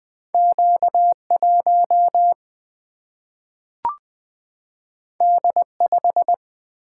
Morse code, heard as a keyed tone, about 20 wpm, 700 Hz. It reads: Q1 D5
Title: Single-Letter Single-Number 20wpm